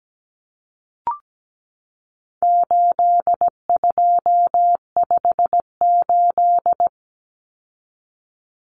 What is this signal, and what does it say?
Morse code, heard as a keyed tone, about 17 wpm, 700 Hz. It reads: 8258
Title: Common Cut Numbers - 4-Digit 17wpm